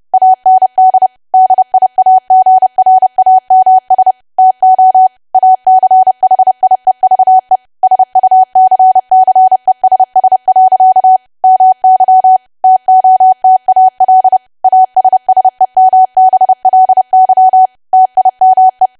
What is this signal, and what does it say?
Morse code, heard as a keyed tone, about 30 wpm, 750 Hz. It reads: AND DIAGRAMS TO ACHIEVE SUCCESS. MY TOTAL ASSEMBLY TIME
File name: cwa / W1AW-Level3 / 30WPM